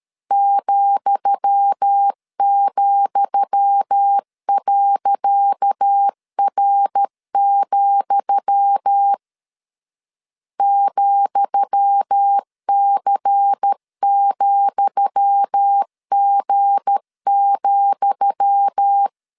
Morse code, heard as a keyed tone, about 13 wpm, 800 Hz. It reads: ,,.R, ,C,G,
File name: LEZIONI / 10 SIGNS / 10D COMA 12wpm 2544